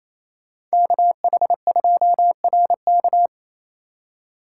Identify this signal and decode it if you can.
Morse code, heard as a keyed tone, about 33 wpm, 700 Hz. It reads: KH2RK
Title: Speed Racing - USA 2x2 Call Signs 28wpm